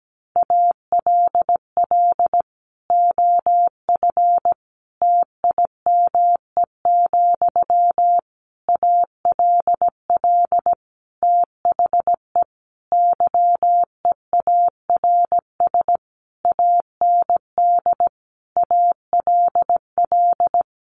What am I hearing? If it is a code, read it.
Morse code, heard as a keyed tone, about 17 wpm, 700 Hz. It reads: ALL OF TIME, ALL THE YEARS AND ALL
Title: A Sound Of Thunder 17wpm